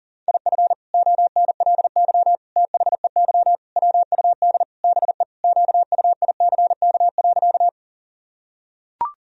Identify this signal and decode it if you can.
Morse code, heard as a keyed tone, about 37 wpm, 700 Hz. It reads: IF ONLY THEY WUD BE QUICK.
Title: Sentences from Top 500 Words and Abbreviations 40wpm